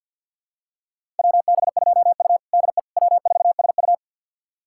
Morse code, heard as a keyed tone, about 51 wpm, 700 Hz. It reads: WB2U DE W4SU
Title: USA Callsign DE Callsign - 4 Characters - 50wpm